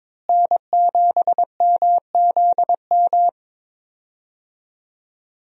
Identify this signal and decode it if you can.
Morse code, heard as a keyed tone, about 22 wpm, 700 Hz. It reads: N7MZM
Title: USA Call Signs - 5 Characters - Repeated 3x - 22wpm